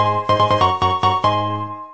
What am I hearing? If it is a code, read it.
There is no code.